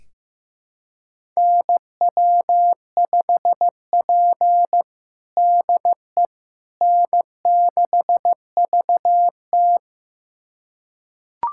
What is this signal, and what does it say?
Morse code, heard as a keyed tone, about 16 wpm, 700 Hz. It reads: NW5P DE N6VT
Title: USA Callsign DE Callsign - 4 Characters - 15wpm